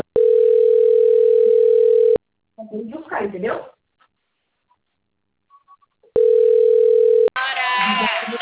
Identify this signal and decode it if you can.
Ringback tone (North America), 440 + 480 Hz, with speech.
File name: out- - - - - .36422